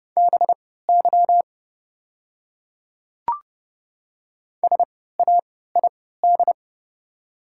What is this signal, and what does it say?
Morse code, heard as a keyed tone, about 31 wpm, spaced as for 17 wpm, 700 Hz. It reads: BY SAID
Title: 3x Character Spacing - Top 100 Words - 30wpm